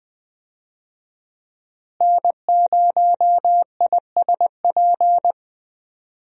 Morse code, heard as a keyed tone, about 20 wpm, 700 Hz. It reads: N0ISP
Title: USA Call Signs - 5 Characters - Repeated 3x - 20wpm